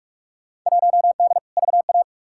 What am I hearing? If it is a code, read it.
Morse code, heard as a keyed tone, about 45 wpm, 700 Hz. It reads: 1D VA
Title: Field Day Exchanges - 45wpm